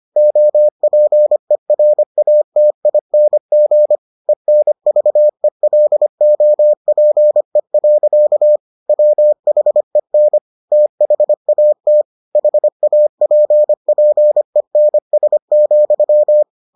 Morse code, heard as a keyed tone, about 25 wpm, 600 Hz. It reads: OPERATING ENVELOPE. WHEN THAT HAPPENS,